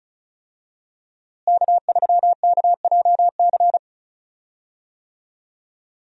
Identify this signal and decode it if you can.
Morse code, heard as a keyed tone, about 36 wpm, 700 Hz. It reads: K3KJC
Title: USA Call Signs - 5 Characters - Repeated 2x - 35wpm